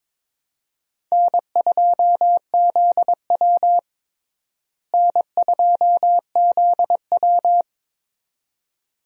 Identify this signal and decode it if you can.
Morse code, heard as a keyed tone, about 22 wpm, 700 Hz. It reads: N2ZW N2ZW
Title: USA Call Signs - 4 Characters - Repeated 2x - 22wpm